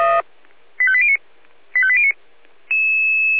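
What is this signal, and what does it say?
Five-tone selective call (ZVEI), heard as quick runs of five tones, tones of about 70 ms. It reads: 85809 85809
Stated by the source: ZVEI five-tone selective call 85244 received by SDR.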